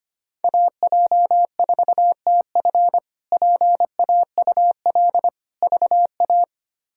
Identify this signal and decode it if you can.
Morse code, heard as a keyed tone, about 25 wpm, 700 Hz. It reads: AJ4TF PAUL VA
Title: K1USN SST QSO Practice - 25wpm